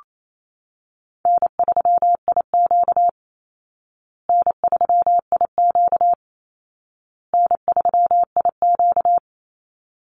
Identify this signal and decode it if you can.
Morse code, heard as a keyed tone, about 28 wpm, 700 Hz. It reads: N3IQ N3IQ N3IQ
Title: Call Signs Repeated 2x - 28wpm